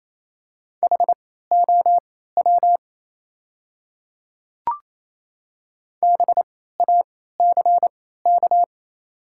Morse code, heard as a keyed tone, about 28 wpm, spaced as for 16 wpm, 700 Hz. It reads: HOW BACK
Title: 3x Character Spacing - Top 300 Words - 28wpm